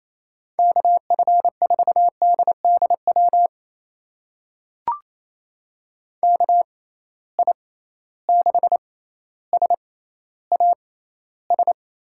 Morse code, heard as a keyed tone, about 28 wpm, spaced as for 14 wpm, 700 Hz. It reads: KF4DDW KI6SAS
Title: USA Call Signs - 6 Characters - 6x Character Spacing - 28wpm